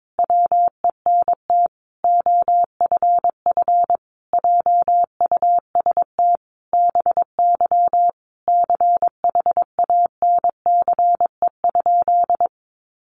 Morse code, heard as a keyed tone, about 22 wpm, 700 Hz. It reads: WENT OFF JUST BY CHANCE?
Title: Sentences from Top 800 Words 22wpm